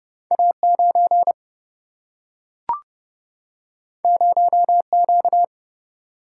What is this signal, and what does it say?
Morse code, heard as a keyed tone, about 30 wpm, 700 Hz. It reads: A9 0Q
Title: Single-Letter Single-Number 30wpm